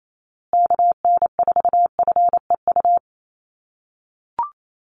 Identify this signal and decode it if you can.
Morse code, heard as a keyed tone, about 28 wpm, 700 Hz. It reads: KN4FEU
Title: USA Call Signs Repeated 2x 28wpm